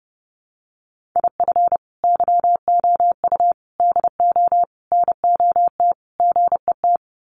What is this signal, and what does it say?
Morse code, heard as a keyed tone, about 30 wpm, 700 Hz. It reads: IF YOU DO NOT GET
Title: Sets of 5 Words Encore2 30wpm